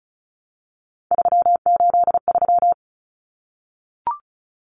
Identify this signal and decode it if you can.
Morse code, heard as a keyed tone, about 34 wpm, 700 Hz. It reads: 383
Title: Numbers - 3-Digit 35wpm